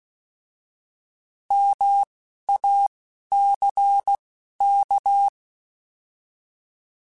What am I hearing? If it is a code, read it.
Morse code, heard as a keyed tone, about 16 wpm, 800 Hz. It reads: M A C K